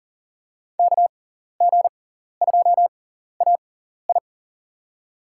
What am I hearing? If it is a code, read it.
Morse code, heard as a keyed tone, about 40 wpm, spaced as for 14 wpm, 700 Hz. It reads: KG2AI
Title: USA Call Signs - 5 Characters - 6x Character Spacing - 40wpm